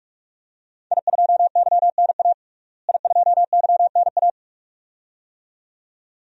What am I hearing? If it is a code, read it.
Morse code, heard as a keyed tone, about 46 wpm, 700 Hz. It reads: I2YNA I2YNA
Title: DXCC-100 Call Signs - 5 Characters - Repeated 2x - 45wpm